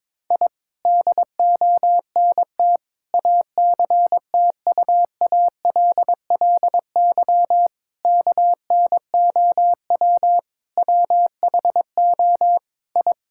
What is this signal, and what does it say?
Morse code, heard as a keyed tone, about 22 wpm, 700 Hz. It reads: I DONT ACTUALLY KNOW WHO I
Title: Sentences from Top 1000 Words - Review Part 5 - 22wpm